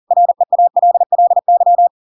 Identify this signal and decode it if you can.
Morse code, heard as a keyed tone, about 40 wpm, 700 Hz. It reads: REALLY
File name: Pinocchio / Pinocchio ch7 40 40 ik1hge